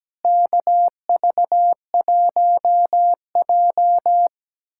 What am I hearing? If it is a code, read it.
Morse code, heard as a keyed tone, about 17 wpm, 700 Hz. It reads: KV1J
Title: K1USN SST QSO Practice - 17wpm